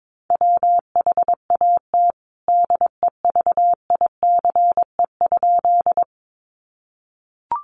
Morse code, heard as a keyed tone, about 22 wpm, 700 Hz. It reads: WHAT DEVICE?
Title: Sentences from Top 1500 Words 22wpm